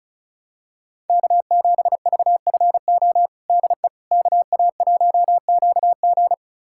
Morse code, heard as a keyed tone, about 34 wpm, 700 Hz. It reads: K7VFO DE KA1QG
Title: USA Callsign DE Callsign - 5 Characters - 35wpm